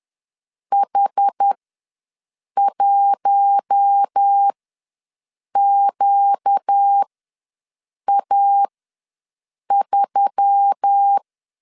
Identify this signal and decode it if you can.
Morse code, heard as a keyed tone, about 10 wpm, spaced as for 6 wpm, 800 Hz. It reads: H1QA3
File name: LEZIONI / 5 LBVC / 5 LB ws 10wpm 3616